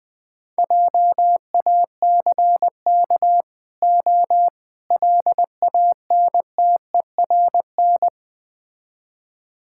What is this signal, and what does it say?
Morse code, heard as a keyed tone, about 21 wpm, 700 Hz. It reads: JACK O LANTERN
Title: Easy Words - Speed-Racing with Hints - 20wpm